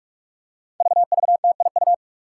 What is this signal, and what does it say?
Morse code, heard as a keyed tone, about 43 wpm, 700 Hz. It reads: VVTIU
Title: Five Challenging Letter-Number Combinations 45wpm